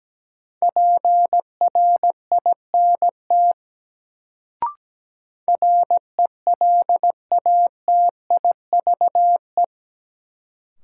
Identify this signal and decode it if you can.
Morse code, heard as a keyed tone, about 17 wpm, 700 Hz. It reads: PRINT RELATIVE
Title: Sentences from Top 1400 Words - New Word Drill - 17wpm